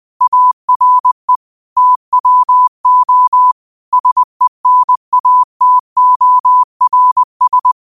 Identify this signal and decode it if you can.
Morse code, heard as a keyed tone, about 20 wpm, 1000 Hz. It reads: ARE TWO SENATORS
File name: qotdfiles / 20wpm